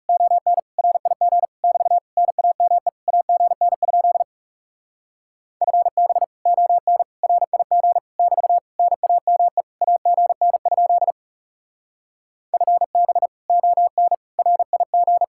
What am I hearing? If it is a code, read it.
Morse code, heard as a keyed tone, about 36 wpm, 700 Hz. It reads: ON RIG <BT> NAME AGN? FB ON RIG <BT> NAME AGN? FB ON RIG
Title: Speed-Racing Common QSO Phrases - 30wpm